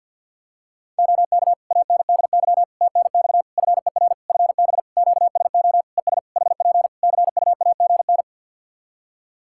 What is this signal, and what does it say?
Morse code, heard as a keyed tone, about 48 wpm, 700 Hz. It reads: OK ANDY TNX FER FB QSO ES HP CUAGN
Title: K1USN SST QSO Practice 50wpm